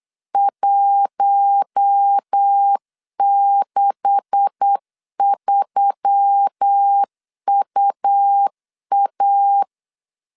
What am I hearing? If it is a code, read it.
Morse code, heard as a keyed tone, about 8 wpm, 800 Hz. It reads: 163UA
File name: LEZIONI / 9 286 8wpm 3425